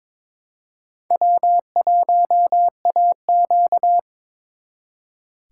Morse code, heard as a keyed tone, about 28 wpm, 700 Hz. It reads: W1AQ
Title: Speed Racing - USA 1x2 Call Signs 22wpm